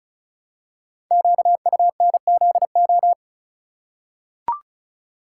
Morse code, heard as a keyed tone, about 36 wpm, 690 Hz. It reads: QUNZO
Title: Code Groups - 5 Letters - 35wpm